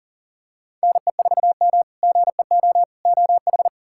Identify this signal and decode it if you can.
Morse code, heard as a keyed tone, about 40 wpm, 700 Hz. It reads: NE4M GEO OH
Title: K1USN SST QSO Practice - 40wpm